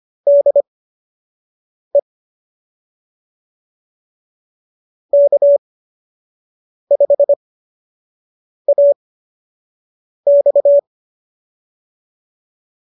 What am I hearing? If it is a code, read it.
Morse code, heard as a keyed tone, about 25 wpm, spaced as for 6 wpm, 590 Hz. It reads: DE K5AX